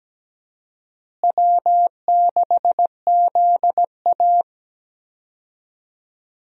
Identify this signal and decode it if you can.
Morse code, heard as a keyed tone, about 17 wpm, 700 Hz. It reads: W6ZA
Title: USA Call Signs - 4 Characters - Repeated 2x - 17wpm